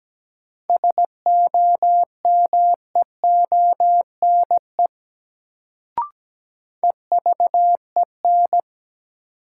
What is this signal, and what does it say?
Morse code, heard as a keyed tone, about 17 wpm, 700 Hz. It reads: SOMEONE EVEN
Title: Sentences from Top 600 Words - New Word Drill - 17wpm